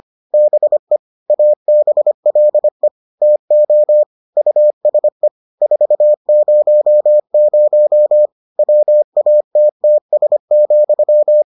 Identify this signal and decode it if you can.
Morse code, heard as a keyed tone, about 25 wpm, 600 Hz. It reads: BE ABLE TO USE 400 WATTS,